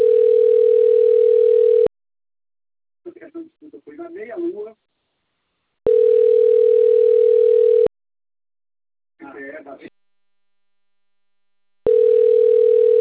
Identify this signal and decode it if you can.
Ringback tone (North America), 440 + 480 Hz, with speech.
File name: rg-1600-unknown- - - .15608